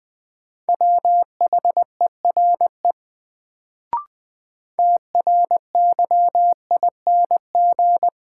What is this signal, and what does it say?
Morse code, heard as a keyed tone, about 20 wpm, 700 Hz. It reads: WHERE TRYING
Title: Sets of 2 Words - New Word Drill - 20wpm